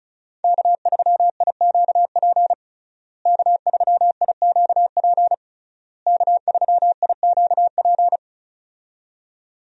Morse code, heard as a keyed tone, about 34 wpm, 700 Hz. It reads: K3IQP K3IQP K3IQP
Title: USA Call Signs - 5 Characters - Repeated 3x - 35wpm